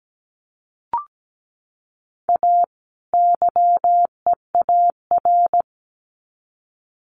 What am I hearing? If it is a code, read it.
Morse code, heard as a keyed tone, about 17 wpm, 700 Hz. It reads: A YEAR
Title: Sets of 2 Words 17wpm